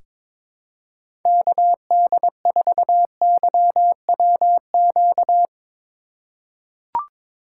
Morse code, heard as a keyed tone, about 23 wpm, 700 Hz. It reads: KD4YWQ
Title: USA Call Signs - 6 Characters - 4x Character Spacing - 22wpm